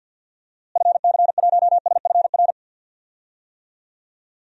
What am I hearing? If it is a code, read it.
Morse code, heard as a keyed tone, about 52 wpm, 700 Hz. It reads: FC1SFR
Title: DXCC-100 Call Signs - 6 Characters - Repeated 3x - 50wpm